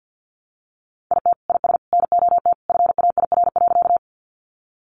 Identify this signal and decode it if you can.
Morse code, heard as a keyed tone, about 50 wpm, 700 Hz. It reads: IT IS NOT FAIR.
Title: Sentences from Top 1300 Words 50wpm